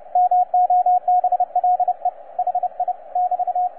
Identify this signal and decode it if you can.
Morse code, heard as a keyed tone, about 30 wpm, 720 Hz, with radio noise. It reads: MOBLE HI <BT>
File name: posts / xw2cw-10-cw